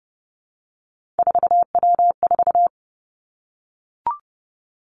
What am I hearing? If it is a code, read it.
Morse code, heard as a keyed tone, about 31 wpm, 700 Hz. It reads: 4W4
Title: Three Challenging Letter-Number Combinations 30wpm